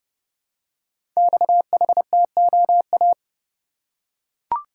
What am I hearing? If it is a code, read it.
Morse code, heard as a keyed tone, about 30 wpm, 700 Hz. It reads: XHTOA
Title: Code Groups - 5 Letters - 30wpm